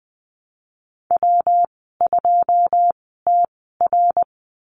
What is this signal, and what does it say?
Morse code, heard as a keyed tone, about 20 wpm, spaced as for 15 wpm, 700 Hz. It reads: W2TR
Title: 128 Lesson 21g - Number 6 - Call Signs 20wpm